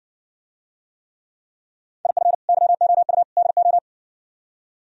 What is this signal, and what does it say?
Morse code, heard as a keyed tone, about 58 wpm, 700 Hz. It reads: IF YOU DO